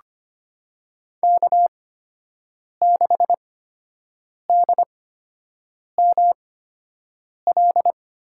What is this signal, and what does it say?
Morse code, heard as a keyed tone, about 25 wpm, spaced as for 7 wpm, 700 Hz. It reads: K6DML
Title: USA Call Signs - 5 Characters - 8x Character Spacing - 25wpm